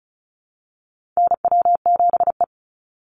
Morse code, heard as a keyed tone, about 34 wpm, 700 Hz. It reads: NW7E
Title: NA SOTA Chaser Callsigns 35wpm